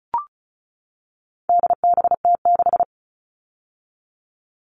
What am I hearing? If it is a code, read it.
Morse code, heard as a keyed tone, about 34 wpm, 700 Hz. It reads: DBT6
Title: All Cut Numbers - 4-Digit 35wpm